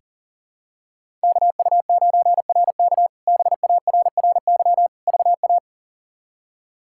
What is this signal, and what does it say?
Morse code, heard as a keyed tone, about 40 wpm, 700 Hz. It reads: KU9RK BARRY VA
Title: K1USN SST QSO Practice - 40wpm